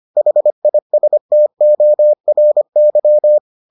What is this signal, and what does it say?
Morse code, heard as a keyed tone, about 25 wpm, 600 Hz. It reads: HISTORY